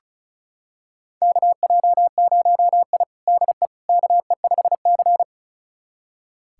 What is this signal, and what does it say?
Morse code, heard as a keyed tone, about 34 wpm, 700 Hz. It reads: KJ0I DE KE5C
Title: USA Callsign DE Callsign - 4 Characters - 35wpm